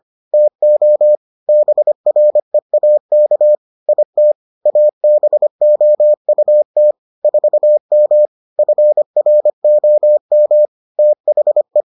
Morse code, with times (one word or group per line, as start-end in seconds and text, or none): 0.3-1.2 TO
1.5-3.5 BREAK
3.9-4.3 IT
4.7-6.9 ABOUT
7.2-8.3 4M
8.6-10.7 FROM
11.0-11.8 THE